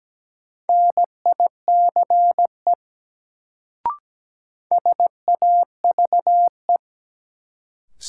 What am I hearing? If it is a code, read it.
Morse code, heard as a keyed tone, about 17 wpm, 700 Hz. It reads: NICE SAVE
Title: Sentences from Top 1100 Words - New Word Drill - 17wpm